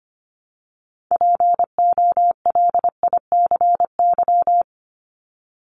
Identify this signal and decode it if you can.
Morse code, heard as a keyed tone, about 25 wpm, 700 Hz. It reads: POLICY
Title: Top 501-750 Words 25wpm